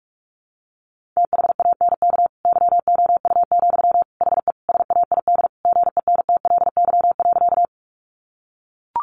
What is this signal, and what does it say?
Morse code, heard as a keyed tone, about 44 wpm, 700 Hz. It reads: THANK YOU, HE SAID GENTLY.
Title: Sentences from Top 1500 Words 45wpm